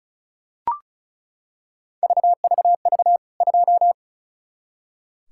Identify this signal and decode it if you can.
Morse code, heard as a keyed tone, about 36 wpm, 700 Hz. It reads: VVV 2
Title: K1USN SST QSO Practice 35wpm